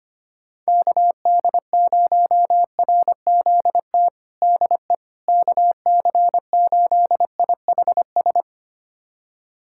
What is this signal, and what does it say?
Morse code, heard as a keyed tone, about 25 wpm, 700 Hz. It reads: KD0RZT DE KC8IHS
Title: USA Callsign DE Callsign 25wpm